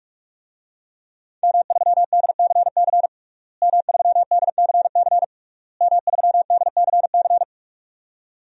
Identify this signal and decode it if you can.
Morse code, heard as a keyed tone, about 46 wpm, 700 Hz. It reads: M3DCC M3DCC M3DCC
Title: DXCC-100 Call Signs - 5 Characters - Repeated 3x - 45wpm